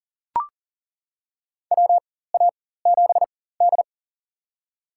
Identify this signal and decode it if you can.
Morse code, heard as a keyed tone, about 40 wpm, spaced as for 19 wpm, 700 Hz. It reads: WA7D
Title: USA Call Signs - 4 Characters - 4x Character Spacing - 40wpm